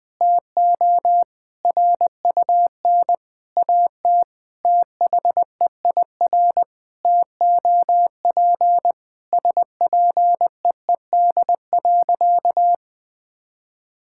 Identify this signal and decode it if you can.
Morse code, heard as a keyed tone, about 20 wpm, 700 Hz. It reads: TO RUN AT THEIR TOP SPEED.
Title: Sentences from Top 1000 Words - Review Part 5 - 20wpm